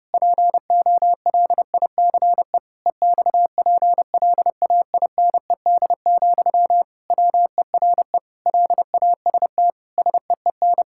Morse code, heard as a keyed tone, about 29 wpm, 700 Hz. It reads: POLICE EXPLAINED, WERE LAST SEEN
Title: Pangram Challenge v2 - 30wpm